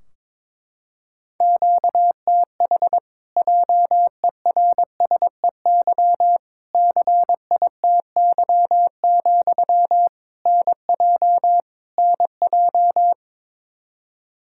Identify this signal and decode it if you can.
Morse code, heard as a keyed tone, about 22 wpm, 700 Hz. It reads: QTH JERSEY CITY, NJ NJ
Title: Practice QSO - Exchange 1 - Beginning - 22wpm